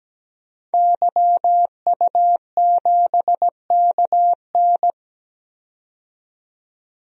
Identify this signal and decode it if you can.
Morse code, heard as a keyed tone, about 17 wpm, 700 Hz. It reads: YU7KN
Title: DXCC-100 Call Signs - 5 Characters - 17wpm